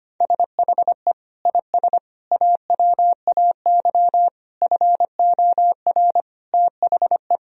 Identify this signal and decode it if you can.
Morse code, heard as a keyed tone, about 25 wpm, 700 Hz. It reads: SHE IS AWAY FOR THE
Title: Sentences from Top 1000 Words - Review Part 2 - 25wpm